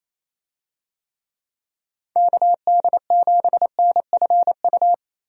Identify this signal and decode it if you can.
Morse code, heard as a keyed tone, about 28 wpm, 700 Hz. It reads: KD7NFU
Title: USA Call Signs - 6 Characters - Repeated 2x - 28wpm